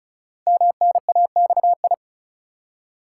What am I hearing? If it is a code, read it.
Morse code, heard as a keyed tone, about 36 wpm, 690 Hz. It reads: MNAXI
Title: Code Groups - 5 Letters - 35wpm